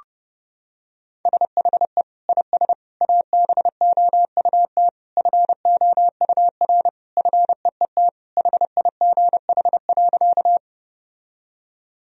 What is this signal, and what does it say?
Morse code, heard as a keyed tone, about 30 wpm, 700 Hz. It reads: SHE IS ABOUT FOUR FEET HIGH.
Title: Sentences from Top 1000 Words - Review Part 1 - 30wpm